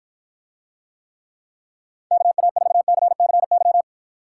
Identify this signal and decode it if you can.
Morse code, heard as a keyed tone, about 53 wpm, 700 Hz. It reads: KA4CCY